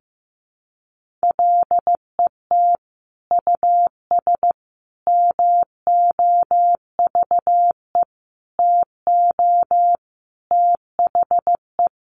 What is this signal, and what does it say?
Morse code, heard as a keyed tone, about 15 wpm, 700 Hz. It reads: LET US MOVE TO THE
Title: Sentences from Top 1500 Words 15wpm